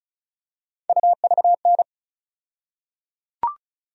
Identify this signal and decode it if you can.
Morse code, heard as a keyed tone, about 34 wpm, 700 Hz. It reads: UVN
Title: USA Call Sign Postfixes - 3 Letters 35wpm